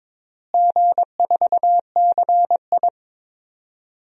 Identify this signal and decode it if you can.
Morse code, heard as a keyed tone, about 21 wpm, 700 Hz. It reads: G4CI
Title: DXCC-100 Call Signs - 4 Characters - Repeated 2x - 22wpm